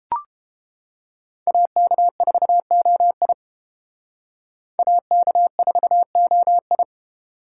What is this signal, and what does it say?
Morse code, heard as a keyed tone, about 31 wpm, 700 Hz. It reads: AK4OI AK4OI